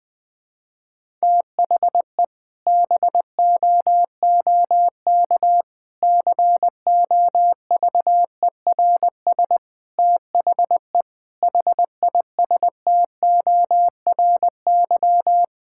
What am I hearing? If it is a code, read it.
Morse code, heard as a keyed tone, about 20 wpm, 700 Hz. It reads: THE BOOK COVERS THE HISTORY
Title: Sentences with 4-Word Phrases - 20wpm